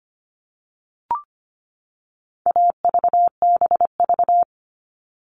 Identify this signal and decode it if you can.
Morse code, heard as a keyed tone, about 25 wpm, 700 Hz. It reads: AVBV
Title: All Cut Numbers - 4-Digit 25wpm